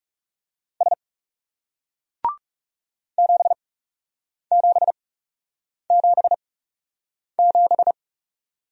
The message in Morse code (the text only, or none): S 7777